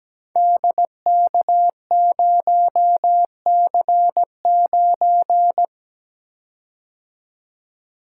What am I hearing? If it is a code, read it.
Morse code, heard as a keyed tone, about 17 wpm, 700 Hz. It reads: DK0C9
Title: Code Groups - 5 Letter:Numbers - 17wpm